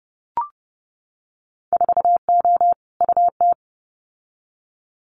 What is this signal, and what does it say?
Morse code, heard as a keyed tone, about 30 wpm, 700 Hz. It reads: 4O UT
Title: Winter Field Day Exchanges Encore 30wpm